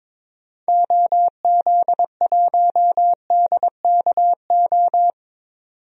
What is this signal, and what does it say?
Morse code, heard as a keyed tone, about 22 wpm, 700 Hz. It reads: OZ1DKO
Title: DXCC-100 Call Signs - 6 Characters - 22wpm